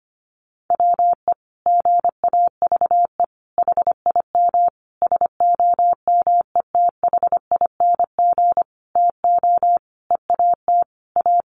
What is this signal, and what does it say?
Morse code, heard as a keyed tone, about 25 wpm, 700 Hz. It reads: WE GAVE HIM SOMETHING TO EAT A
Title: Sentences from Top 600 Words 25wpm